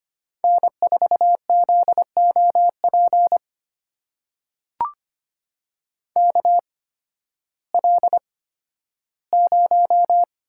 Morse code, heard as a keyed tone, about 25 wpm, spaced as for 13 wpm, 700 Hz. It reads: N4ZOP KL0
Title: USA Call Signs - 8x Character Spacing - 25wpm